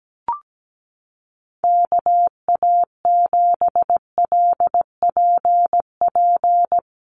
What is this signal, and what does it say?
Morse code, heard as a keyed tone, about 17 wpm, 700 Hz. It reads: KA7LPP
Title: USA Call Signs Repeated 2x 17wpm